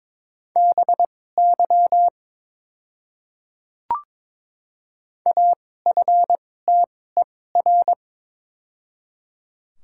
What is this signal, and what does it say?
Morse code, heard as a keyed tone, about 22 wpm, spaced as for 16 wpm, 700 Hz. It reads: BY AFTER
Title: 2x Character Spacing - Top 200 Words - 22wpm